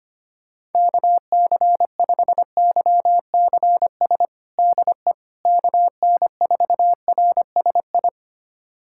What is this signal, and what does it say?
Morse code, heard as a keyed tone, about 25 wpm, 700 Hz. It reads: KC5YCS DE KN4RSI
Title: USA Callsign DE Callsign - 6 Characters - 25wpm